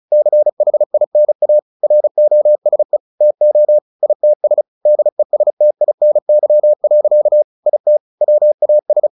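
Morse code, heard as a keyed tone, about 34 wpm, 600 Hz. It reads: CHINA ROSE TO ITS DESTINY. IT WAS
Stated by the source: Morse practice at 35 wpm from CW med Gnister, sending Ken Follett's novel Never in English.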